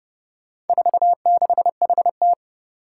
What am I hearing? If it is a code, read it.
Morse code, heard as a keyed tone, about 30 wpm, 700 Hz. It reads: 46HT
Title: Four Challenging Letter-Number Combinations 30wpm